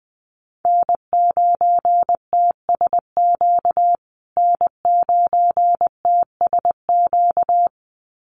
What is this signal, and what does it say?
Morse code, heard as a keyed tone, about 20 wpm, 700 Hz. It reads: N9TSQ N9TSQ
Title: USA Call Signs Repeated 2x 20wpm